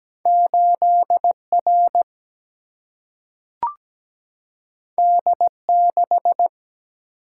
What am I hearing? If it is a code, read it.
Morse code, heard as a keyed tone, about 17 wpm, 700 Hz. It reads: 8R D6
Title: Single-Letter Single-Number 17wpm